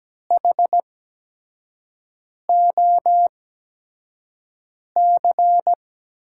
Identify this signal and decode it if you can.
Morse code, heard as a keyed tone, about 17 wpm, spaced as for 5 wpm, 700 Hz. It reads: HOC
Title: USA Call Signs - 5 Characters - 8x Character Spacing - 17wpm